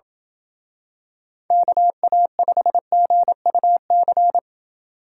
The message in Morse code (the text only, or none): KA5GUC